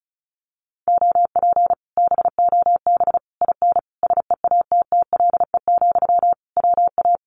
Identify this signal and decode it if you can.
Morse code, heard as a keyed tone, about 36 wpm, 700 Hz. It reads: OP BOB IN SEATTLE, WA
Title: Common QSO Phrases Encore 35wpm